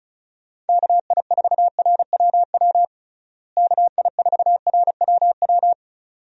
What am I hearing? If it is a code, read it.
Morse code, heard as a keyed tone, about 36 wpm, 700 Hz. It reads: KI4RWW KI4RWW
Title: USA Call Signs - 6 Characters - Repeated 2x - 35wpm